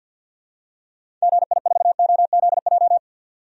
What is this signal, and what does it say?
Morse code, heard as a keyed tone, about 52 wpm, 700 Hz. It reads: GI4OZJ
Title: DXCC-100 Call Signs - 6 Characters - 50wpm